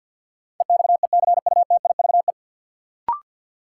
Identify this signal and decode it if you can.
Morse code, heard as a keyed tone, about 53 wpm, 700 Hz. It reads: EXECUTIVE
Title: Sets of 5 Words Encore - New Word Drill - 50wpm